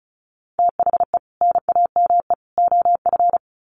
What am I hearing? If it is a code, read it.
Morse code, heard as a keyed tone, about 34 wpm, 700 Hz. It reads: THE NAME OF
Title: Sets of 3 Words 35wpm